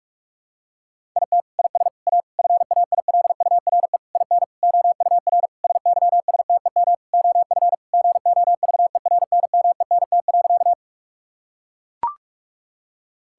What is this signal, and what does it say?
Morse code, heard as a keyed tone, about 41 wpm, 700 Hz. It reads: IT IS A FAILURE IN OUR SYSTEM OF GOVERNMENT.
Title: Sentences from Top 1000 Words - Review Part 6 - 45wpm